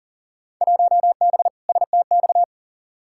Morse code, heard as a keyed tone, about 40 wpm, 700 Hz. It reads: 1B STX
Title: Field Day Exchanges - 40wpm